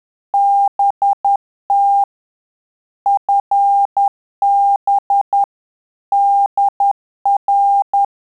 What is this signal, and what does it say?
Morse code, heard as a keyed tone, about 10 wpm, 800 Hz. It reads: BT FB DR
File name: Telegraphy-group / QSO CW 50 CPM